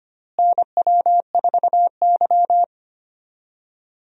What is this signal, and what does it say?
Morse code, heard as a keyed tone, about 25 wpm, 700 Hz. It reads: NW4Y
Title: USA Call Signs - 4 Characters - Repeated 2x - 25wpm